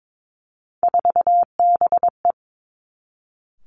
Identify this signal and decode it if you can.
Morse code, heard as a keyed tone, about 22 wpm, 700 Hz. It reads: 4BE
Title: Three Challenging Letter-Number Combinations 22wpm